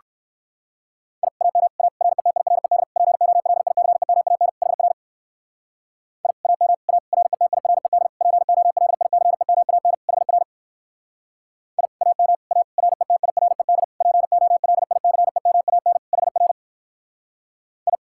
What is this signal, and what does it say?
Morse code, heard as a keyed tone, about 58 wpm, 700 Hz. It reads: I AM A RETIRED POLICEMAN HR I AM A RETIRED POLICEMAN HR I AM A RETIRED POLICEMAN HR I